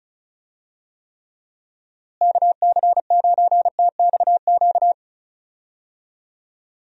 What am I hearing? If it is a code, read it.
Morse code, heard as a keyed tone, about 36 wpm, 700 Hz. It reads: KC9TXQ
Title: USA Call Signs - 6 Characters - Repeated 2x - 35wpm